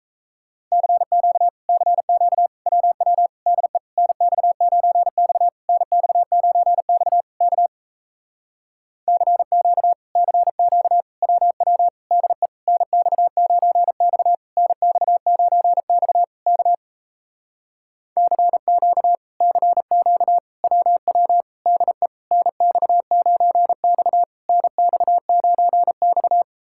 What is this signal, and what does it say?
Morse code, heard as a keyed tone, about 38 wpm, 700 Hz. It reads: CQ CQ WW DE NX9X NX9X K CQ CQ WW DE NX9X NX9X K CQ CQ WW DE NX9X NX9X